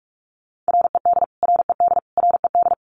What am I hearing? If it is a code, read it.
Morse code, heard as a keyed tone, about 45 wpm, 700 Hz. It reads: RED RED RED
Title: Top 500 Words Repeated 45wpm